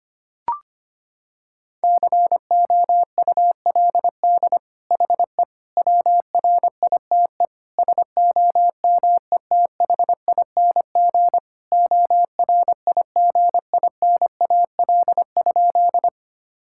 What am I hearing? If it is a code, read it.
Morse code, heard as a keyed tone, about 25 wpm, 700 Hz. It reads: COULD HE WRITE SOMETHING ORIGINAL?
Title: Sentences from Top 1000 Words - Review Part 6 - 25wpm